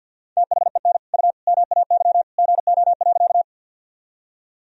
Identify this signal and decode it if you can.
Morse code, heard as a keyed tone, about 51 wpm, 700 Hz. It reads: THEN U MAY GO.
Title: Sentences from Top 100 Words and Abbreviations 50wpm